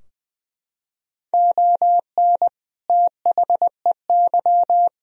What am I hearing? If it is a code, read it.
Morse code, heard as a keyed tone, about 20 wpm, 700 Hz. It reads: ON THEY
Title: Sentences from Top 1000 Words - Review Part 2 - 20wpm